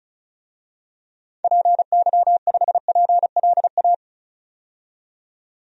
Morse code, heard as a keyed tone, about 34 wpm, 700 Hz. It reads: PY5PLA
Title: DXCC-100 Call Signs - 6 Characters - 35wpm